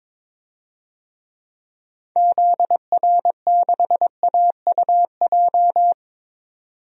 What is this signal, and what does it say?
Morse code, heard as a keyed tone, about 22 wpm, 700 Hz. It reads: ZR6AUJ
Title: DXCC-100 Call Signs - 6 Characters - 22wpm